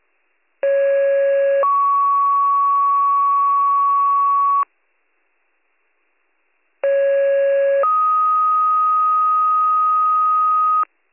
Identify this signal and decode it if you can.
Two-tone page, heard as a one-second tone, then a three-second tone, about 550 then 1120 Hz, about 550 then 1250 Hz. It is heard from 0.6 to 10.8 s.